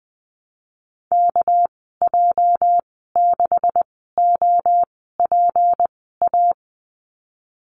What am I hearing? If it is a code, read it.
Morse code, heard as a keyed tone, about 20 wpm, spaced as for 14 wpm, 700 Hz. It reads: KJ6OPA
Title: 238 Lesson 38g - Letter q - Call Signs 20wpm